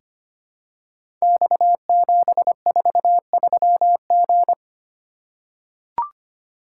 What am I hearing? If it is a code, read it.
Morse code, heard as a keyed tone, about 25 wpm, 700 Hz. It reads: X743G
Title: Code Groups - 5 Letter:Numbers - 25wpm